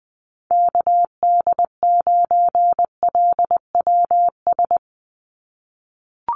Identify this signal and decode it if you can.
Morse code, heard as a keyed tone, about 19 wpm, 700 Hz. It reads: KD9LWS
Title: USA Call Signs 20wpm